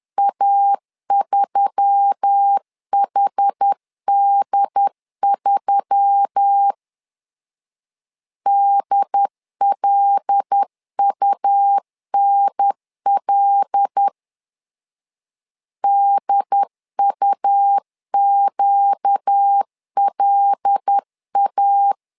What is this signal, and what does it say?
Morse code, heard as a keyed tone, about 10 wpm, 800 Hz. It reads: A3HD3 DLUNL DUQLA
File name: LEZIONI / 5 LBVC / 5 L 10wpm 2259